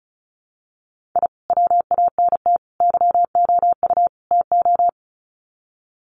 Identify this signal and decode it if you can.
Morse code, heard as a keyed tone, about 34 wpm, 700 Hz. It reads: I WANT YOU TO
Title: Sets of 4 Words 35wpm